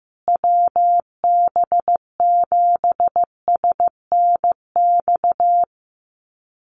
Morse code, heard as a keyed tone, about 15 wpm, 700 Hz. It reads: WB7SNX
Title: USA Call Signs Repeated 3x 15wpm